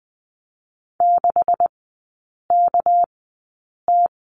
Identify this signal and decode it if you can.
Morse code, heard as a keyed tone, about 20 wpm, spaced as for 8 wpm, 700 Hz. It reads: 6KT
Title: 198 Lesson 32e - Letter k - Three Characters 20wpm